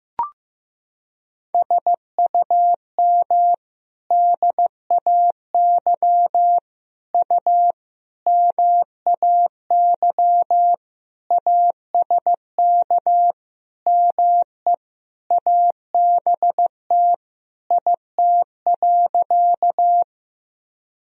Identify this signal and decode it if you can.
Morse code, heard as a keyed tone, about 15 wpm, 700 Hz. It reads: SUM DAY U MAY ASK ME ABT IT.
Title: Sentences from Top 200 Words and Abbreviations 15wpm